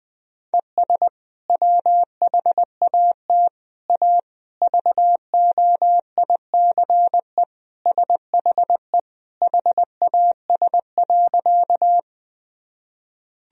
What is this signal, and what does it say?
Morse code, heard as a keyed tone, about 20 wpm, 700 Hz. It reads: ES WHAT A VOICE SHE HAS.
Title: Sentences from Top 500 Words and Abbreviations 20wpm